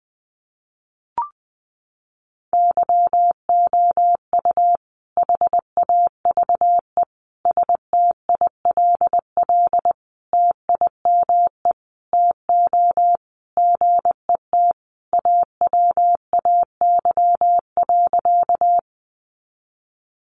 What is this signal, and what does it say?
Morse code, heard as a keyed tone, about 20 wpm, 700 Hz. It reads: YOU HAVE STILL TIME TO GET AWAY.
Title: Sentences from Top 600 Words 20wpm